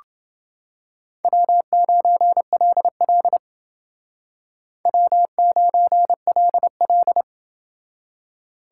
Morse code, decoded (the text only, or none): W9LL W9LL